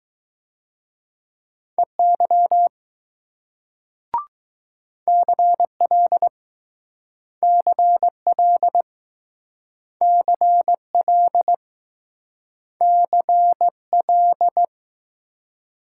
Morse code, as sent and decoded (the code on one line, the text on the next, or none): . -.-- / -.-. .-.. / -.-. .-.. / -.-. .-.. / -.-. .-..
EY CL CL CL CL